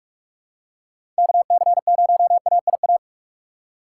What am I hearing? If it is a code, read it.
Morse code, heard as a keyed tone, about 42 wpm, 700 Hz. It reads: KC0AIA
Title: USA Call Signs - 2x Character Spacing - 45wpm